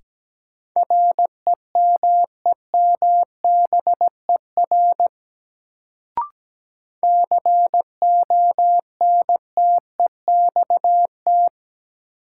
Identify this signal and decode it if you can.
Morse code, heard as a keyed tone, about 17 wpm, 700 Hz. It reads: REMEMBER CONTEXT
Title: Sets of 4 Words - New Word Drill - 17wpm